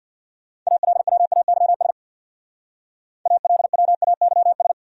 Morse code, heard as a keyed tone, about 59 wpm, 700 Hz. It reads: ALWAYS ALWAYS
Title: Speed-Racing Top 500 Words 50wpm